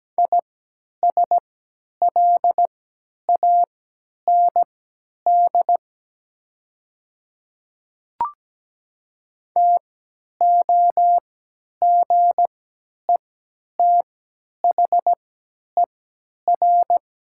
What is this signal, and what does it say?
Morse code, heard as a keyed tone, about 17 wpm, spaced as for 10 wpm, 700 Hz. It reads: ISLAND TOGETHER
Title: 3x Character Spacing - Top 500 Words - 17wpm